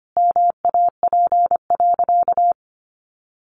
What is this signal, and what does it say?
Morse code, heard as a keyed tone, about 25 wpm, 700 Hz. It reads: MAP.
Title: Sentences from Top 400 Words 25wpm